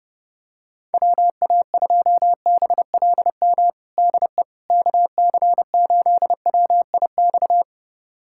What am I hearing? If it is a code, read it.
Morse code, heard as a keyed tone, about 30 wpm, 690 Hz. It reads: WA2BLM DE KC8WIX
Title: USA Callsign DE Callsign 30wpm